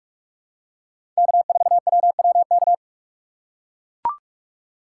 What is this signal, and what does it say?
Morse code, heard as a keyed tone, about 43 wpm, 700 Hz. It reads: K4WWK
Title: USA Call Signs - 5 Characters - 3x Character Spacing - 45wpm